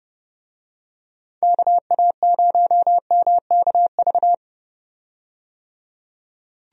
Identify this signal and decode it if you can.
Morse code, heard as a keyed tone, about 30 wpm, 700 Hz. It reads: KA0MKV
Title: USA Call Signs - 6 Characters - 30wpm